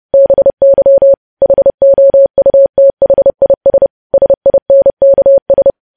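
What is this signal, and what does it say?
Morse code, heard as a keyed tone, about 30 wpm, 550 Hz. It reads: BY HOUTHIS SINKS
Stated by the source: Morse code practice of news headlines at 30.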